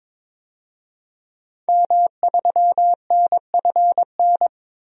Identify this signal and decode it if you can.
Morse code, heard as a keyed tone, about 21 wpm, 700 Hz. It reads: M3NFN
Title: DXCC-100 Call Signs - 5 Characters - 22wpm